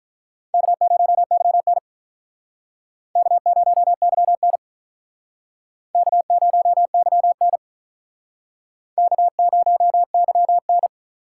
Morse code, heard as a keyed tone, about 45 wpm, 700 Hz. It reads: K0YN K0YN K0YN K0YN